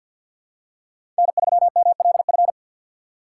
Morse code, heard as a keyed tone, about 50 wpm, 700 Hz. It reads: N3MLF
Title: USA Call Signs - 6x Character Spacing - 50wpm